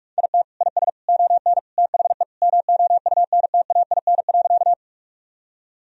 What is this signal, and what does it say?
Morse code, heard as a keyed tone, about 42 wpm, 700 Hz. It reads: IT IS ON THE MOUNTAIN.
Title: Sentences from Top 1000 Words - Review Part 1 - 45wpm